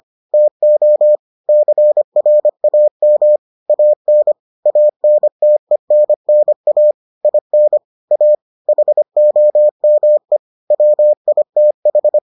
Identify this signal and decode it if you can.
Morse code, heard as a keyed tone, about 25 wpm, 600 Hz. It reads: TO CRAM AN ANTENNA IN A HOME WITH